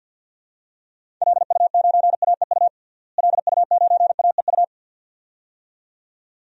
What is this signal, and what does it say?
Morse code, heard as a keyed tone, about 52 wpm, 700 Hz. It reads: LU9AEU LU9AEU
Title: DXCC-100 Call Signs - 6 Characters - Repeated 2x - 50wpm